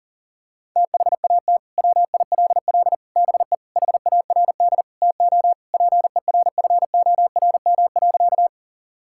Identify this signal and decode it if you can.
Morse code, heard as a keyed tone, about 37 wpm, 700 Hz. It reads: THAT WILL BE HARD TO PERFORM.
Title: Sentences from Top 1000 Words - Review Part 5 - 40wpm